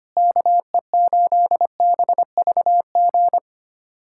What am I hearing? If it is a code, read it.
Morse code, heard as a keyed tone, about 25 wpm, 700 Hz. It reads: KE8BVG
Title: USA Callsign DE Callsign - 6 Characters - 25wpm